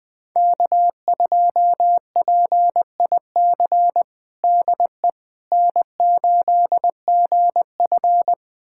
Morse code, heard as a keyed tone, about 20 wpm, 700 Hz. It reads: K2PIC DE N8GF
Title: USA Callsign DE Callsign 20wpm